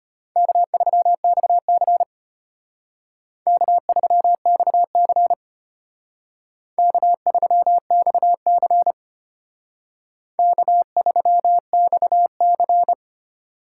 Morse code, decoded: K3XC K3XC K3XC K3XC

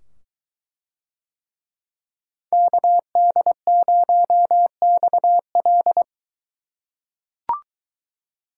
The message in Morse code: KD0XL

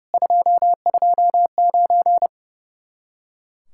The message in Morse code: 229

